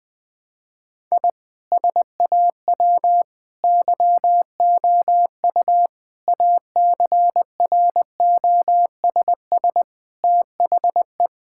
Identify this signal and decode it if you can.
Morse code, heard as a keyed tone, about 20 wpm, 700 Hz. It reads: I SAW YOU ACROSS THE
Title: Sentences from Top 1000 Words - Review Part 5 - 20wpm